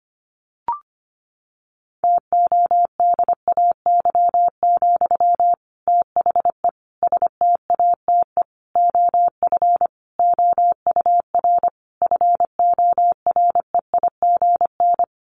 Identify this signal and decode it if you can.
Morse code, heard as a keyed tone, about 25 wpm, 700 Hz. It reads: TODAY, THE STATE OF OUR FOREIGN
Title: Sentences from Top 1400 Words 25wpm